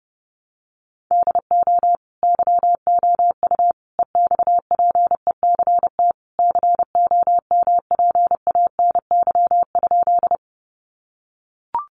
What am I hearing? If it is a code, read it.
Morse code, heard as a keyed tone, about 28 wpm, 700 Hz. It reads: DO YOU EXPECT COMPANY?
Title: Sentences from Top 1100 Words 30wpm